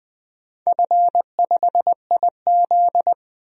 Morse code, heard as a keyed tone, about 20 wpm, 700 Hz. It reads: F5IZ
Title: DXCC-100 Call Signs - 4 Characters - 20wpm